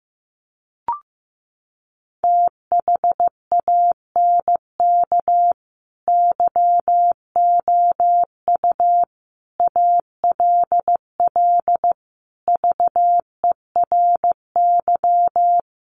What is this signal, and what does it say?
Morse code, heard as a keyed tone, about 15 wpm, 700 Hz. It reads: THANK YOU ALL VERY
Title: Sets of 5 Words Encore 15wpm